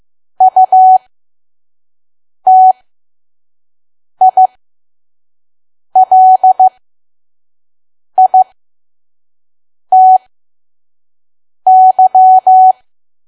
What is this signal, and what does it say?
Morse code, heard as a keyed tone, about 15 wpm, spaced as for 5 wpm, 750 Hz. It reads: UTILITY
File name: Archive / 5 WPM files / PM